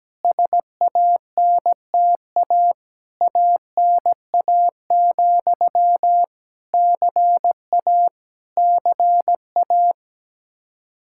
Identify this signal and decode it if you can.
Morse code, heard as a keyed tone, about 17 wpm, 700 Hz. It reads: SANTA ANA, CA CA
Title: Practice QSO - Exchange 1 - Beginning - 17wpm